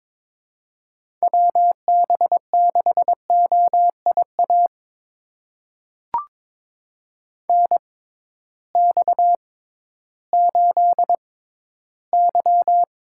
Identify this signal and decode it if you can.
Morse code, heard as a keyed tone, about 22 wpm, spaced as for 13 wpm, 700 Hz. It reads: WB6OIA NX8Y
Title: USA Call Signs - 6x Character Spacing - 22wpm